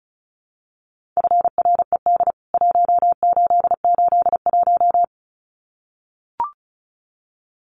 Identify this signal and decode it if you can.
Morse code, heard as a keyed tone, about 34 wpm, 700 Hz. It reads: FRED 1881
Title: CWT Exchange 35wpm